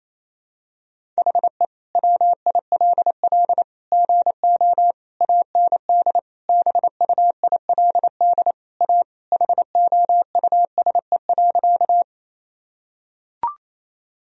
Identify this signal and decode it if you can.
Morse code, heard as a keyed tone, about 28 wpm, 700 Hz. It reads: HE WILL GO AND BUILD A HOUSE.
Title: Sentences from Top 1000 Words - Review Part 1 - 28wpm